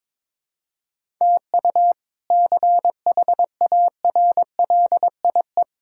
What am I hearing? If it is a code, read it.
Morse code, heard as a keyed tone, about 22 wpm, 700 Hz. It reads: TU CHARLIE
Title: K1USN SST QSO Practice - 22wpm